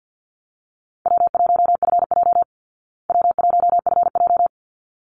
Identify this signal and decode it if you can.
Morse code, heard as a keyed tone, about 49 wpm, 700 Hz. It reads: W1FJ W1FJ
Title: Easy Call Signs Repeated 2x - 50wpm